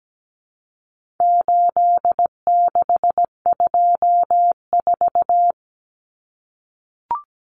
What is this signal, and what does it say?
Morse code, heard as a keyed tone, about 17 wpm, 700 Hz. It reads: 8624
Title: Common Cut Numbers - 4-Digit 17wpm